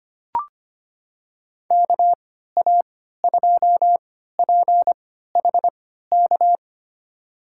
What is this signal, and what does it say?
Morse code, heard as a keyed tone, about 25 wpm, spaced as for 14 wpm, 700 Hz. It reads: KA2PHK
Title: USA Call Signs - 6 Characters - 3x Character Spacing - 25wpm